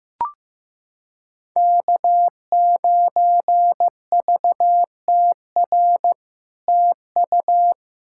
Morse code, heard as a keyed tone, about 15 wpm, 700 Hz. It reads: K9VTR TU
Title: SOTA QSO Practice 15wpm